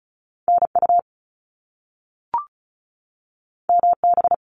Morse code, heard as a keyed tone, about 36 wpm, 700 Hz. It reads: NU MB
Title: Canadian Provinces and Territories 35wpm